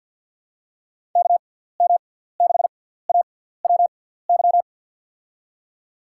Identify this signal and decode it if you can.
Morse code, heard as a keyed tone, about 52 wpm, spaced as for 17 wpm, 690 Hz. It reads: KM6AWY
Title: USA Call Signs - 6x Character Spacing - 50wpm